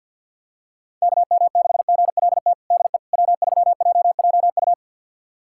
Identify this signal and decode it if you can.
Morse code, heard as a keyed tone, about 52 wpm, 700 Hz. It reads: KM6GLT DE W3JJU
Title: USA Callsign DE Callsign 50wpm